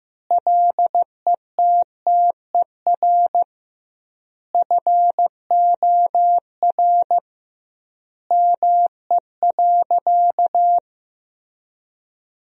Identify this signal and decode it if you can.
Morse code, heard as a keyed tone, about 15 wpm, 700 Hz. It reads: LETTER FOR ME.
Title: 2x Word-Spacing - Sentences from Top 300 Words - 15wpm